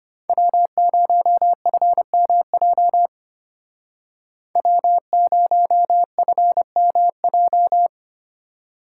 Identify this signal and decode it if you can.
Morse code, heard as a keyed tone, about 32 wpm, 700 Hz. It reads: W0FMJ W0FMJ